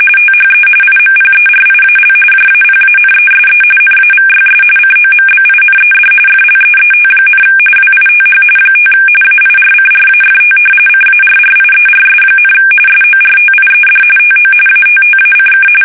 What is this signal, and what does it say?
RTTY, about 45.45 baud. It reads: THE IMPERIAL JAPANESE NAVYS COMBINED FLEET, WITH 5 AIRCRAFT CARRIERS, SORTIED TO ATTACK
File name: fsk850